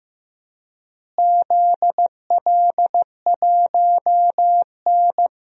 Morse code, heard as a keyed tone, about 15 wpm, 700 Hz. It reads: ZL1N
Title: DXCC-100 Call Signs - 4 Characters - Repeated 3x - 15wpm